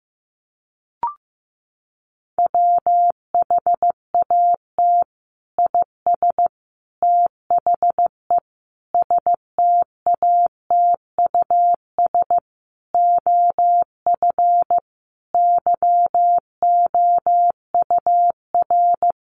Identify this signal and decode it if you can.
Morse code, heard as a keyed tone, about 15 wpm, 700 Hz. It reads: WHAT IS THE STATUS OF YOUR
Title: Sentences from Top 1400 Words 15wpm